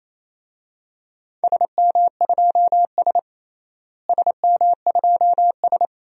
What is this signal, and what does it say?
Morse code, heard as a keyed tone, about 28 wpm, 700 Hz. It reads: SM2S SM2S
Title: DXCC-100 Call Signs - 4 Characters - Repeated 2x - 28wpm